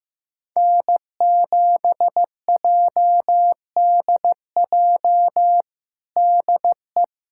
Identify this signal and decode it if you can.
Morse code, heard as a keyed tone, about 15 wpm, 700 Hz. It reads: N7JDJ DE
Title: USA Callsign DE Callsign 15wpm